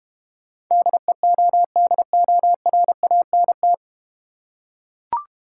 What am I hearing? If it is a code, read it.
Morse code, heard as a keyed tone, about 32 wpm, 700 Hz. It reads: DEODORANT